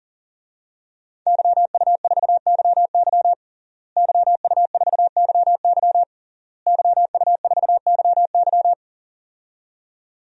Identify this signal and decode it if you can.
Morse code, heard as a keyed tone, about 40 wpm, 700 Hz. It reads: YU4YY YU4YY YU4YY
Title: DXCC-100 Call Signs - 5 Characters - Repeated 3x - 40wpm